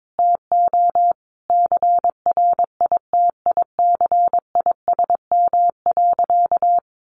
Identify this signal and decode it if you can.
Morse code, heard as a keyed tone, about 22 wpm, 700 Hz. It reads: TO CRITICISM.
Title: Sentences from Top 1300 Words 22wpm